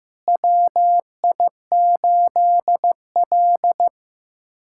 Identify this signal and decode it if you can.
Morse code, heard as a keyed tone, about 15 wpm, 700 Hz. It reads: WI8L
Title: USA Callsign DE Callsign - 4 Characters - 15wpm